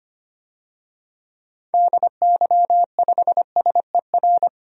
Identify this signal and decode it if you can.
Morse code, heard as a keyed tone, about 25 wpm, 700 Hz. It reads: DY5SER
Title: DXCC-100 Call Signs - 6 Characters - 25wpm